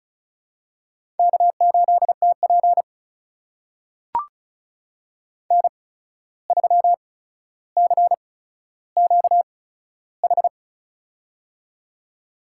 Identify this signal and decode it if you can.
Morse code, heard as a keyed tone, about 36 wpm, spaced as for 14 wpm, 700 Hz. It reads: K8TP N3CQH
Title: USA Call Signs - 8x Character Spacing - 35wpm